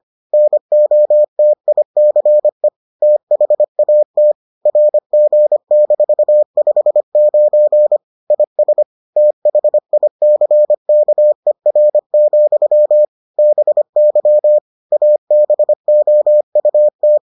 Morse code, heard as a keyed tone, about 25 wpm, 600 Hz. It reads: NOTICE THAT RG-59 IS THICKER, BY ABOUT